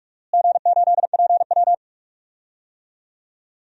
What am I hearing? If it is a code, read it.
Morse code, heard as a keyed tone, about 44 wpm, 700 Hz. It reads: G8PW
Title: DXCC-100 Call Signs - 4 Characters - Repeated 3x - 45wpm